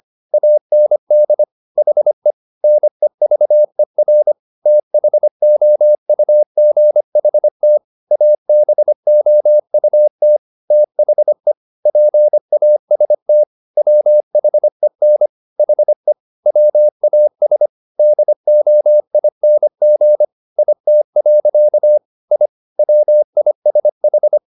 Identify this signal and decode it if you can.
Morse code, heard as a keyed tone, about 25 wpm, 600 Hz. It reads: AND HE NEVER THOUGHT ABOUT THE PAST WHEN HE WAS DOING IT. I WISH
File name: the old man and the sea part 1 25wpm